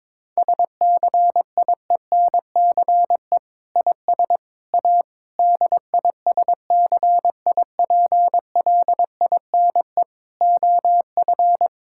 Morse code, heard as a keyed tone, about 22 wpm, 700 Hz. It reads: SCIENCE IS A DISCIPLINE OF
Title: Sentences from Top 1000 Words - Review Part 6 - 22wpm